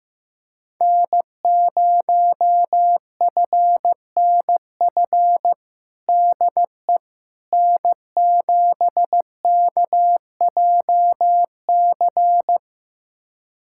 Morse code, heard as a keyed tone, about 15 wpm, 700 Hz. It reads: N0FNF DE N7KJC
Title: USA Callsign DE Callsign - 5 Characters - 15wpm